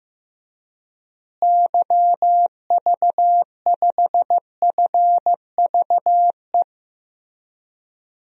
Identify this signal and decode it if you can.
Morse code, heard as a keyed tone, about 16 wpm, 700 Hz. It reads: YV5FVE
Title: DXCC-100 Call Signs - 6 Characters - Repeated 3x - 15wpm